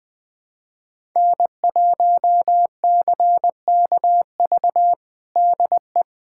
Morse code, heard as a keyed tone, about 21 wpm, 700 Hz. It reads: N1CKV DE
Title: USA Callsign DE Callsign - 5 Characters - 20wpm